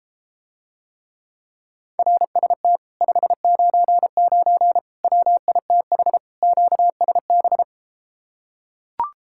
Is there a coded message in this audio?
Yes: Morse code from 2.0 to 7.6 s, heard as a keyed tone, about 33 wpm, 700 Hz. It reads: RST 599 WITH QSB